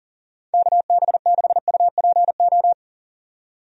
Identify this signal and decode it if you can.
Morse code, heard as a keyed tone, about 40 wpm, 700 Hz. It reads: KB6UPO
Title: USA Call Signs - 6 Characters - 40wpm